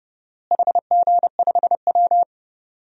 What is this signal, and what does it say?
Morse code, heard as a keyed tone, about 32 wpm, 700 Hz. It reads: HG5W
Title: DXCC-100 Call Signs - 4 Characters - Repeated 2x - 30wpm